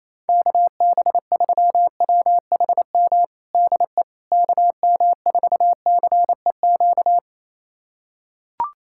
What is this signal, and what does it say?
Morse code, heard as a keyed tone, about 27 wpm, 700 Hz. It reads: KB3WHM DE KM4CEQ
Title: USA Callsign DE Callsign - 6 Characters - 28wpm